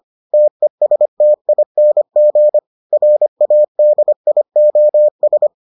Morse code, heard as a keyed tone, about 25 wpm, 600 Hz. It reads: TESTING RADIOS